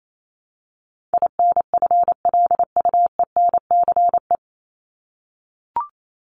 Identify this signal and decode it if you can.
Morse code, heard as a keyed tone, about 28 wpm, 700 Hz. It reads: INFLUENCE
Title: Top 1001-1250 Words 28wpm